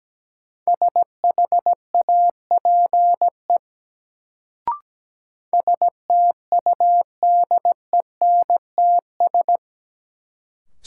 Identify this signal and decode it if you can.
Morse code, heard as a keyed tone, about 17 wpm, 700 Hz. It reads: SHAPE STUDENTS
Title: Sets of 5 Words Encore - New Word Drill - 17wpm